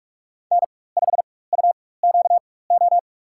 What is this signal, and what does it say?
Morse code, heard as a keyed tone, about 46 wpm, spaced as for 21 wpm, 690 Hz. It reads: N5UQO
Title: USA Call Signs - 5 Characters - 4x Character Spacing - 45wpm